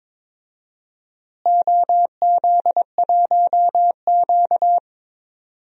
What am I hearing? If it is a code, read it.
Morse code, heard as a keyed tone, about 22 wpm, 700 Hz. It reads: OZ1Q
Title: DXCC-100 Call Signs - 4 Characters - Repeated 3x - 22wpm